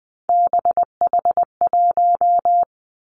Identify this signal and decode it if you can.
Morse code, heard as a keyed tone, about 20 wpm, 700 Hz. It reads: BH1
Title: Three Challenging Letter-Number Combinations 20wpm